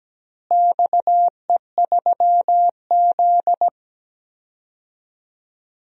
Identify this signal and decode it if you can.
Morse code, heard as a keyed tone, about 17 wpm, 700 Hz. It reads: XE3Z
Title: DXCC-100 Call Signs - 4 Characters - Repeated 2x - 17wpm